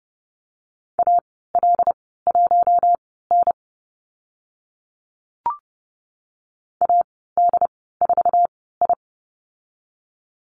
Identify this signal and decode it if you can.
Morse code, heard as a keyed tone, about 30 wpm, spaced as for 17 wpm, 700 Hz. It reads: AL1N AD4I
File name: Lesson 14g - Number 2 - Call Signs 30wpm